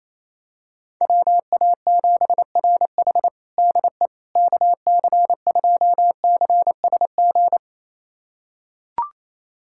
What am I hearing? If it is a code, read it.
Morse code, heard as a keyed tone, about 28 wpm, 700 Hz. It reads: WA7RH DE KC2CSG
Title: USA Callsign DE Callsign 28wpm